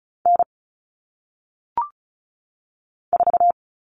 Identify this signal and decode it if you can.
Morse code, heard as a keyed tone, about 36 wpm, 700 Hz. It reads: N 4
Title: All Cut Numbers - 1-Digit 35wpm